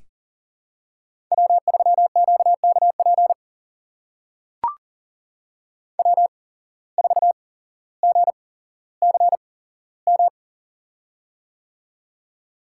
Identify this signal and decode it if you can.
Morse code, heard as a keyed tone, about 40 wpm, spaced as for 17 wpm, 700 Hz. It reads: W3QKP W4GCM
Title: USA Call Signs - 8x Character Spacing - 40wpm